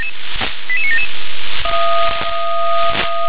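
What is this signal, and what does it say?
Five-tone selective call (ZVEI), heard as quick runs of five tones, tones of about 70 ms. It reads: 88977
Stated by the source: ZVEI five-tone selective call received by SDR.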